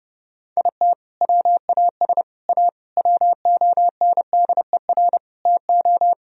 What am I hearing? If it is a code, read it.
Morse code, heard as a keyed tone, about 30 wpm, 700 Hz. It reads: IT WAS A WONDER TO
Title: Sentences from Top 1000 Words - Review Part 3 - 30wpm